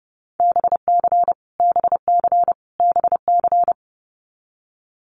Morse code, heard as a keyed tone, about 30 wpm, 700 Hz. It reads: BC BC BC
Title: Canadian Provinces and Territories - Repeated 3x - 30wpm